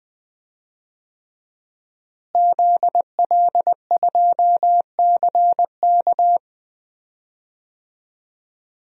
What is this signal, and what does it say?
Morse code, heard as a keyed tone, about 20 wpm, 700 Hz. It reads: ZL2CK
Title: DXCC-100 Call Signs - 5 Characters - Repeated 3x - 20wpm